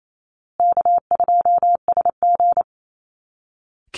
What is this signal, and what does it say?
Morse code, heard as a keyed tone, about 28 wpm, 700 Hz. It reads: K2SG
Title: Easy Call Signs Repeated 3x - 28wpm